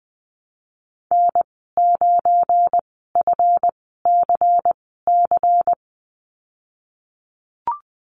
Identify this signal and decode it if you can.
Morse code, heard as a keyed tone, about 20 wpm, spaced as for 15 wpm, 700 Hz. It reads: N9FCC
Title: 207 Lesson 33g - Letter j - Call Signs 20wpm